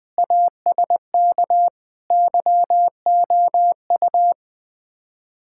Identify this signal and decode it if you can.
Morse code, heard as a keyed tone, about 20 wpm, 700 Hz. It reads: ASK YOU
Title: Build A Phrase - 5-Word Phrases - 20wpm